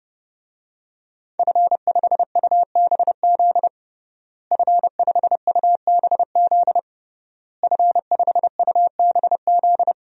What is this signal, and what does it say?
Morse code, heard as a keyed tone, about 30 wpm, 700 Hz. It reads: F5UBZ F5UBZ F5UBZ
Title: DXCC-100 Call Signs - 5 Characters - Repeated 3x - 30wpm